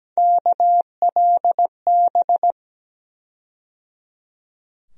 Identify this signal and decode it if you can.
Morse code, heard as a keyed tone, about 17 wpm, 700 Hz. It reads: KLB
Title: USA Call Sign Postfixes - 3 Letters 17wpm